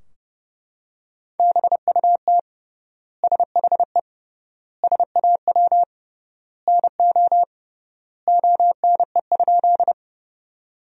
Morse code, heard as a keyed tone, about 30 wpm, 700 Hz. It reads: BUT SHE SAW NO ONE?
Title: 3x Word-Spacing - Sentences from Top 300 Words - 30wpm